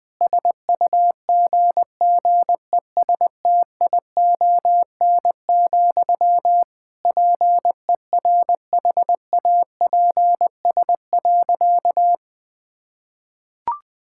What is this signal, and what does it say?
Morse code, heard as a keyed tone, about 20 wpm, 700 Hz. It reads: SUGGESTION, PERHAPS.
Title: Sentences from Top 1600 Words 20wpm